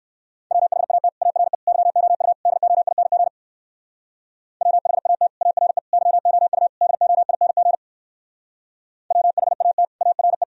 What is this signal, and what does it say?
Morse code, heard as a keyed tone, about 56 wpm, 700 Hz. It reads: WHAT ARE YOU DOING WHAT ARE YOU DOING WHAT ARE
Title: Speed-Racing - Sets of 4 Words 45wpm